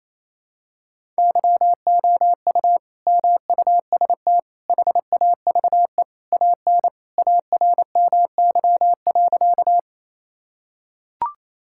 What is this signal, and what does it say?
Morse code, heard as a keyed tone, about 28 wpm, 700 Hz. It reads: YOU MUST HAVE AN ARMY.
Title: Sentences from Top 1000 Words - Review Part 5 - 28wpm